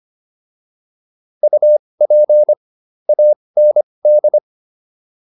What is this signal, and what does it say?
Morse code, heard as a keyed tone, about 25 wpm, spaced as for 20 wpm, 600 Hz. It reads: UP AND